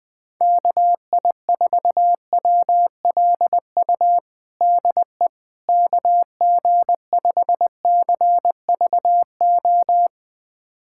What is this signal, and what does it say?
Morse code, heard as a keyed tone, about 20 wpm, 700 Hz. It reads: KI4WLU DE KG5CVO
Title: USA Callsign DE Callsign - 6 Characters - 20wpm